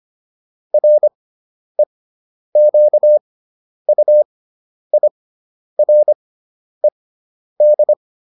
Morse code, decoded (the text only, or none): REQUIRED